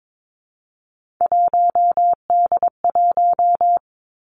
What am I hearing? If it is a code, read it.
Morse code, heard as a keyed tone, about 22 wpm, 700 Hz. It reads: 1D1
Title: Three Challenging Letter-Number Combinations 22wpm